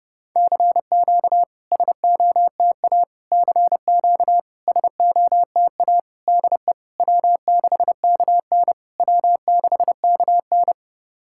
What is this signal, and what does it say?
Morse code, heard as a keyed tone, about 30 wpm, 700 Hz. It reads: CQ SOTA CQ SOTA DE W6KN W6KN
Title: SOTA QSO Practice 30wpm